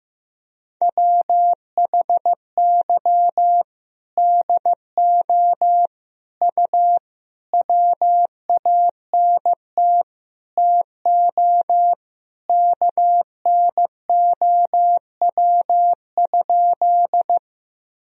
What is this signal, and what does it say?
Morse code, heard as a keyed tone, about 15 wpm, 700 Hz. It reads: WHY DO U WANT TO KNOW?
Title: Sentences from Top 200 Words and Abbreviations 15wpm